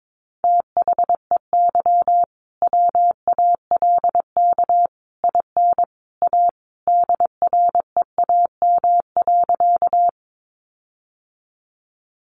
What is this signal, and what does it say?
Morse code, heard as a keyed tone, about 22 wpm, 700 Hz. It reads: THEY WALK IN A DREAM.
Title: Sentences from Top 1500 Words 22wpm